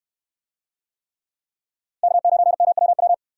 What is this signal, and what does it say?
Morse code, heard as a keyed tone, about 64 wpm, 700 Hz. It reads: K9MWW